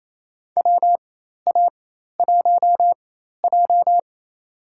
Morse code, heard as a keyed tone, about 28 wpm, spaced as for 13 wpm, 700 Hz. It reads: WA1J
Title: USA Call Signs - 4 Characters - 4x Character Spacing - 28wpm